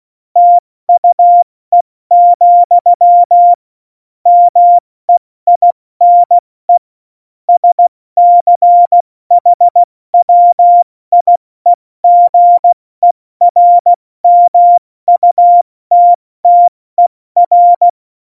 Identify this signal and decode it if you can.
Morse code, heard as a keyed tone, about 16 wpm, 700 Hz. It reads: TUE, MEINE SCHWIEGERMUTTER
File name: Morsekurs / Witze / Witz 18. Juni 2019, 70 F80